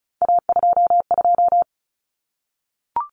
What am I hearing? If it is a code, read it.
Morse code, heard as a keyed tone, about 34 wpm, 700 Hz. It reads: A22
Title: Common Cut Numbers - 3-Digit 35wpm